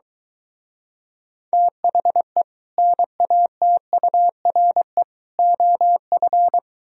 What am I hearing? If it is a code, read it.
Morse code, heard as a keyed tone, about 23 wpm, 700 Hz. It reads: THE NATURE OF